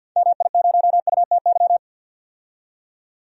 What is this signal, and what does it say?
Morse code, heard as a keyed tone, about 52 wpm, 700 Hz. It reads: MI0UTY
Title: DXCC-100 Call Signs - 6 Characters - 50wpm